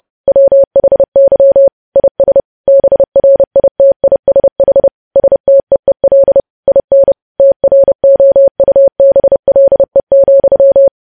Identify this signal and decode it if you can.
Morse code, heard as a keyed tone, about 30 wpm, 550 Hz. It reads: WHY IS BRITISH STEEL IN TROUBLE,
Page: news headlines in morse code at 30 wpm